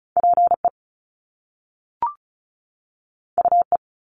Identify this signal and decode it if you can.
Morse code, heard as a keyed tone, about 36 wpm, 700 Hz. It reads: PE UE
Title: Top Two Letter Combinations in Words 35wpm